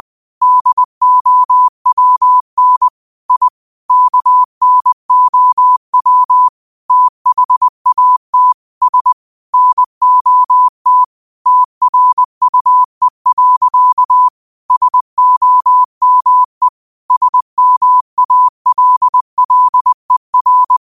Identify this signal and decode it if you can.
Morse code, heard as a keyed tone, about 20 wpm, 1000 Hz. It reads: DOWN I KNOW THAT S NOT TRUE. SOME SMALLER